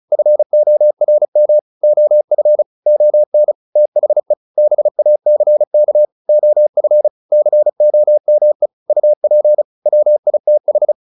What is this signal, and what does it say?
Morse code, heard as a keyed tone, about 34 wpm, 600 Hz. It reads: FORM OF ON THE BACK OF COME UP WITH